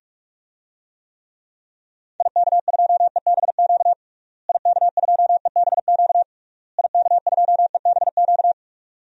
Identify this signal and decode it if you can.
Morse code, heard as a keyed tone, about 46 wpm, 700 Hz. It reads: IK2EBQ IK2EBQ IK2EBQ
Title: DXCC-100 Call Signs - 6 Characters - Repeated 3x - 45wpm